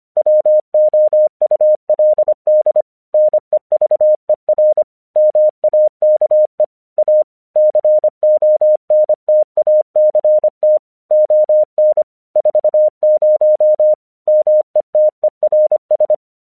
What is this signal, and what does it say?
Morse code, heard as a keyed tone, about 24 wpm, 610 Hz. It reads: WOULD NEVER MAKE A CONTACT ON 40 METERS
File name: 1 English 25wpm / AQC26